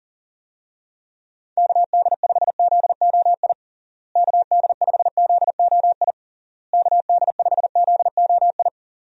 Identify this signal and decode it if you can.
Morse code, heard as a keyed tone, about 37 wpm, 700 Hz. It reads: KD5ZOI KD5ZOI KD5ZOI
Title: USA Call Signs - 6 Characters - Repeated 3x - 40wpm